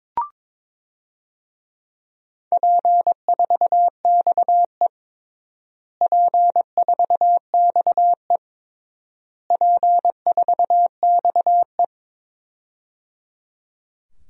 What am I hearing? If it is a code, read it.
Morse code, heard as a keyed tone, about 21 wpm, 700 Hz. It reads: P4XE P4XE P4XE
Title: DXCC-100 Call Signs - 4 Characters - Repeated 3x - 22wpm